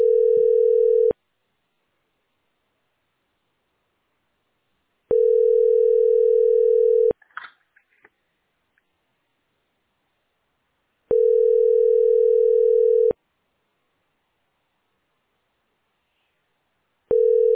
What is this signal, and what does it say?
Ringback tone (North America), 440 + 480 Hz.